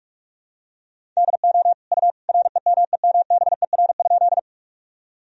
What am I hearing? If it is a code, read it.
Morse code, heard as a keyed tone, about 46 wpm, 700 Hz. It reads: DO U REMEMBER?
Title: Sentences from Top 400 Words and Abbreviations 45wpm